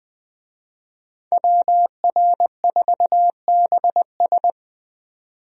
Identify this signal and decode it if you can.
Morse code, heard as a keyed tone, about 20 wpm, 700 Hz. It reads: WR4BS
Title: USA Call Signs - 5 Characters - 20wpm